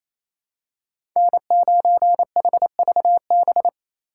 Morse code, heard as a keyed tone, about 28 wpm, 700 Hz. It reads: N9HVB
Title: USA Call Signs - 5 Characters - 2x Character Spacing - 28wpm